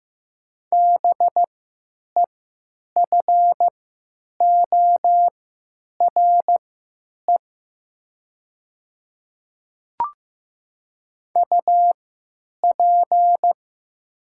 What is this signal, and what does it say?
Morse code, heard as a keyed tone, about 15 wpm, spaced as for 9 wpm, 700 Hz. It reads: BEFORE UP
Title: 3x Character Spacing - Top 500 Words - 15wpm